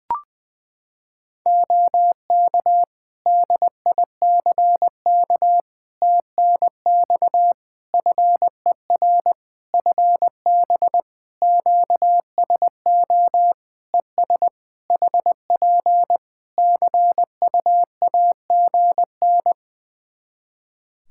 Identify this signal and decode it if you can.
Morse code, heard as a keyed tone, about 20 wpm, 700 Hz. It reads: OK DICK TNX FER FB QSO ES HP CUAGN
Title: Practice QSO - Exchange 4 - Ending - 20wpm